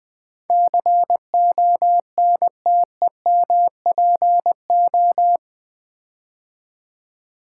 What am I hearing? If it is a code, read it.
Morse code, heard as a keyed tone, about 20 wpm, 700 Hz. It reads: CONTEMPO
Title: Build A Word - 4 Letters At A Time - 12-Letter Words - 20wpm